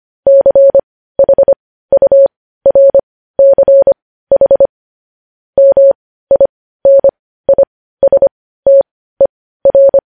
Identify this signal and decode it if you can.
Morse code, heard as a keyed tone, about 25 wpm, spaced as for 15 wpm, 550 Hz. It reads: CHURCH MINISTER
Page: news-headlines-in-morse-code-at-15-wpm